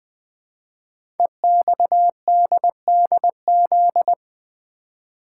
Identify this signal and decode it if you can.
Morse code, heard as a keyed tone, about 20 wpm, 700 Hz. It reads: EXDDZ
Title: Code Groups - 5 Letters - 20wpm